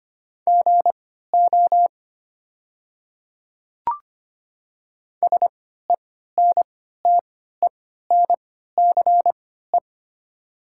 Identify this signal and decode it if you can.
Morse code, heard as a keyed tone, about 25 wpm, spaced as for 14 wpm, 700 Hz. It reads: GO SENTENCE
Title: 3x Character Spacing - Top 200 Words - 25wpm